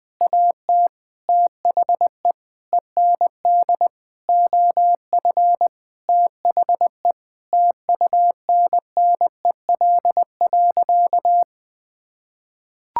Morse code, heard as a keyed tone, about 21 wpm, 700 Hz. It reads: AT THE END OF THE TUNNEL.
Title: Sentences with 8-Word Phrases - 20wpm